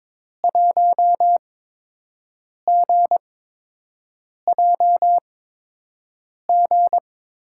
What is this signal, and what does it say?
Morse code, heard as a keyed tone, about 22 wpm, spaced as for 6 wpm, 700 Hz. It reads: 1GJG
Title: USA Call Signs - 5 Characters - 8x Character Spacing - 22wpm